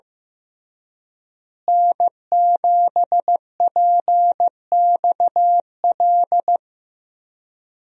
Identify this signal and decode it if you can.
Morse code, heard as a keyed tone, about 15 wpm, 700 Hz. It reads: N7PXL
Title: USA Call Signs - 5 Characters - Repeated 3x - 15wpm